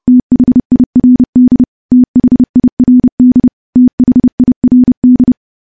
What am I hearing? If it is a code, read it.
Morse code, heard as a keyed tone, about 32 wpm, 250 Hz. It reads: THIRD THIRD THIRD
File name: morse / morse-training-30WPM